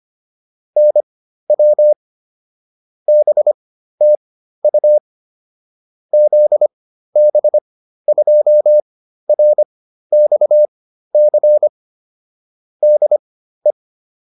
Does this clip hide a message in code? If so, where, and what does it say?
Yes: Morse code from 0.8 to 13.7 s, heard as a keyed tone, about 25 wpm, spaced as for 13 wpm, 600 Hz. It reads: NW BTU ZB2RXC DE